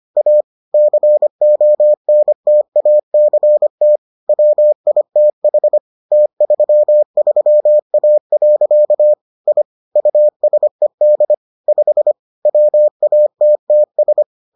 Morse code, heard as a keyed tone, about 25 wpm, 600 Hz. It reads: A CONTACT WITH T33A. I USED 5 WATTS